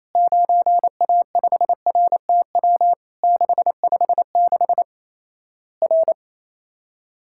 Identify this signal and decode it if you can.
Morse code, heard as a keyed tone, about 28 wpm, 700 Hz. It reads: 9A5RTW 656 R
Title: Worked All Europe DX Contest - QTC Exchange - 28wpm